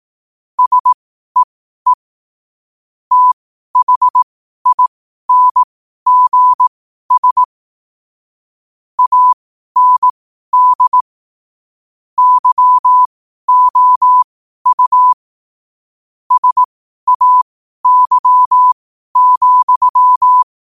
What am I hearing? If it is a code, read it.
Morse code, heard as a keyed tone, about 18 wpm, spaced as for 12 wpm, 1000 Hz. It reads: SEE THINGS AND YOU SAY,
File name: qotdfiles / 12wpm